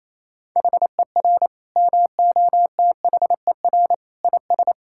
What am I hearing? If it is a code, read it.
Morse code, heard as a keyed tone, about 28 wpm, 700 Hz. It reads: HER MOTHER IS
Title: Sentences from Top 1000 Words - Review Part 5 - 28wpm